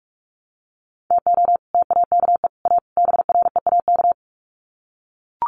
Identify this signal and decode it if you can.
Morse code, heard as a keyed tone, about 44 wpm, 700 Hz. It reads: TO TAKE A BREAK
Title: Sets of 4 Words 45wpm